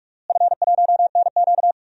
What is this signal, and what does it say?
Morse code, heard as a keyed tone, about 46 wpm, 700 Hz. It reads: F1NQ
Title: DXCC-100 Call Signs - 4 Characters - Repeated 2x - 45wpm